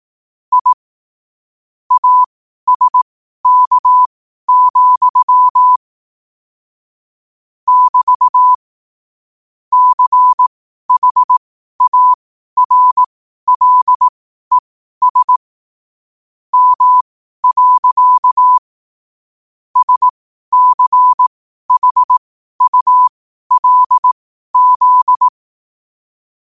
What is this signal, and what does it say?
Morse code, heard as a keyed tone, about 18 wpm, spaced as for 12 wpm, 1000 Hz. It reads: I ASK, <BT> CHARLES M. SCHULZ
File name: qotdfiles / 12wpm